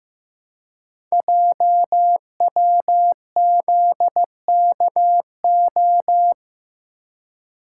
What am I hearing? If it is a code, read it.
Morse code, heard as a keyed tone, about 15 wpm, 700 Hz. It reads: JWZKO
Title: Code Groups - 5 Letters - 15wpm